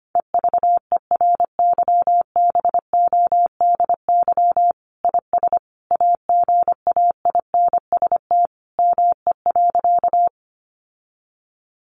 Morse code, heard as a keyed tone, about 25 wpm, 700 Hz. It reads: EVERYBODY IS AGAINST ME.
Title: Sentences from Top 1400 Words 25wpm